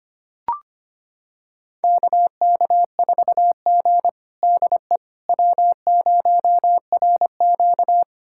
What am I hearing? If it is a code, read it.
Morse code, heard as a keyed tone, about 25 wpm, 700 Hz. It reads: KK4G DE W0RQ
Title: USA Callsign DE Callsign - 4 Characters - 25wpm